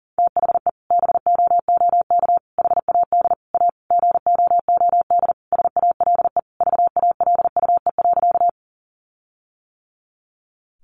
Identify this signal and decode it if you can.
Morse code, heard as a keyed tone, about 40 wpm, 700 Hz. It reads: THE BOOK HAD A GOOD SALE VALUE.
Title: Sentences from Top 1400 Words 40wpm